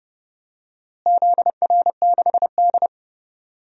Morse code, heard as a keyed tone, about 31 wpm, 700 Hz. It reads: ZR6D
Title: DXCC-100 Call Signs - 4 Characters - Repeated 2x - 30wpm